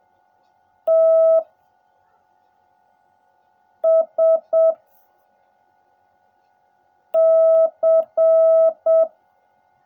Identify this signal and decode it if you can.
Morse code, heard as a keyed tone, about 7 wpm, 650 Hz. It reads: T S C
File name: jh8cbh / 2021cw / cw16a